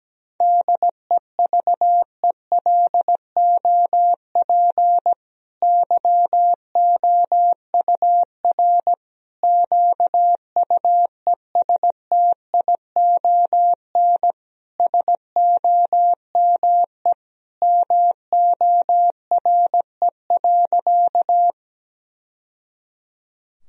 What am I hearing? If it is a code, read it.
Morse code, heard as a keyed tone, about 17 wpm, 700 Hz. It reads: DEVELOP YOUR QUESTION SOME MORE.
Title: Sentences from Top 1000 Words - Review Part 2 - 17wpm